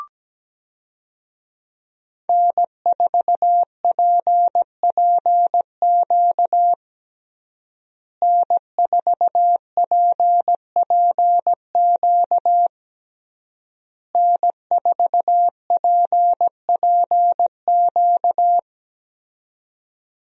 Morse code, heard as a keyed tone, about 17 wpm, 700 Hz. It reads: N4PPQ N4PPQ N4PPQ
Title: USA Call Signs - 5 Characters - Repeated 3x - 17wpm